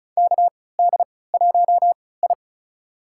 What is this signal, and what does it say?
Morse code, heard as a keyed tone, about 36 wpm, spaced as for 20 wpm, 700 Hz. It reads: KD1I
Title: USA Call Signs - 3x Character Spacing - 35wpm